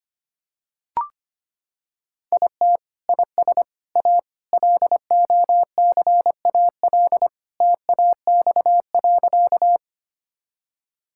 Morse code, heard as a keyed tone, about 25 wpm, 700 Hz. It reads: IT IS A LOCAL TAX.
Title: Sentences from Top 1000 Words - Review Part 5 - 25wpm